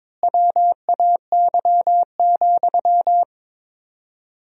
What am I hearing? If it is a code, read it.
Morse code, heard as a keyed tone, about 22 wpm, 700 Hz. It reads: WAY,
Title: 4x Word-Spacing - Sentences from Top 400 Words - 22wpm